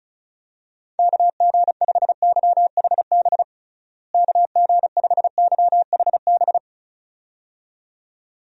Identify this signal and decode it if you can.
Morse code, heard as a keyed tone, about 36 wpm, 700 Hz. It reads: KG5YHB KG5YHB
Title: USA Call Signs - 6 Characters - Repeated 2x - 35wpm